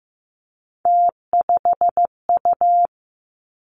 Morse code, heard as a keyed tone, about 15 wpm, 700 Hz. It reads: T5U
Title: Three Challenging Letter-Number Combinations 15wpm